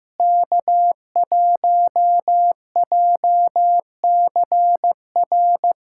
Morse code, heard as a keyed tone, about 15 wpm, 700 Hz. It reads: K1JCR
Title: K1USN SST QSO Practice - 15wpm